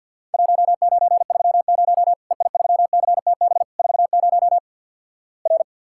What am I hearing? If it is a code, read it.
Morse code, heard as a keyed tone, about 50 wpm, 700 Hz. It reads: 1930 EI3CTB 40 R
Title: Worked All Europe DX Contest - QTC Exchange with Repeats - 50wpm